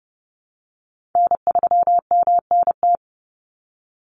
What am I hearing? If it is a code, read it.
Morse code, heard as a keyed tone, about 30 wpm, 700 Hz. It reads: N3MNT
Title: Call Signs Repeated 3x - 30wpm